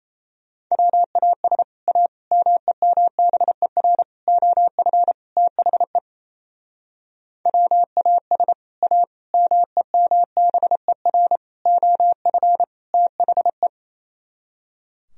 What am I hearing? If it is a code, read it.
Morse code, heard as a keyed tone, about 36 wpm, 700 Hz. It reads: WAS A MEMBER OF THE WAS A MEMBER OF THE